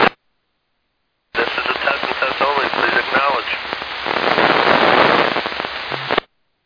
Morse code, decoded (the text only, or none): none